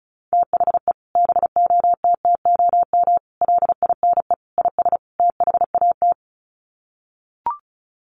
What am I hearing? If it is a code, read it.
Morse code, heard as a keyed tone, about 34 wpm, 700 Hz. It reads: THE BOTTOM LINE IS THAT
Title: Sets of 5 Words Encore 35wpm